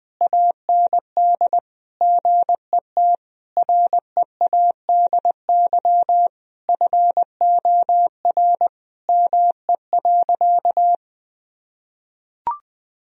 Morse code, heard as a keyed tone, about 21 wpm, 700 Hz. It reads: AND GET READY FOR ME.
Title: Sentences from Top 1000 Words - Review Part 1 - 20wpm